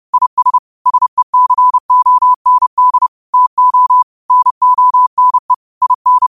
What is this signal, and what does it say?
Morse code, heard as a keyed tone, about 30 wpm, 1000 Hz. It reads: IS SECOND TO NONE IN